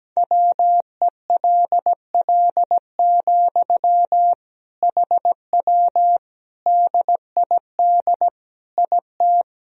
Morse code, heard as a keyed tone, about 17 wpm, 700 Hz. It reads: WELL, HW DID IT
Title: Sentences from Top 200 Words and Abbreviations 17wpm